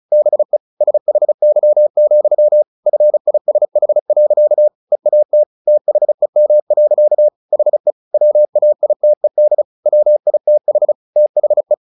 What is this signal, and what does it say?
Morse code, heard as a keyed tone, about 34 wpm, 600 Hz. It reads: BE SHY, FISH. EAT THEM. HE WAITED WITH THE
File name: the old man and the sea / the old man and the sea 1 35wpm